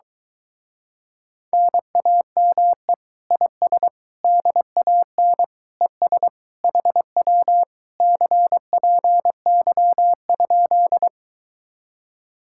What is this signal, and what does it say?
Morse code, heard as a keyed tone, about 23 wpm, 700 Hz. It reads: NAME IS DAN ES HW CPY?